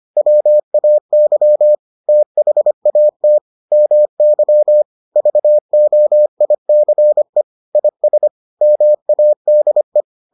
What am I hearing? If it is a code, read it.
Morse code, heard as a keyed tone, about 25 wpm, 600 Hz. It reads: WAY THAT MY VOICE IS MADE